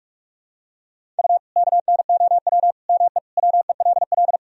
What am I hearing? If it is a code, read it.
Morse code, heard as a keyed tone, about 46 wpm, 700 Hz. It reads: U KNOW ME WELL
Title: Sentences from Top 400 Words and Abbreviations 45wpm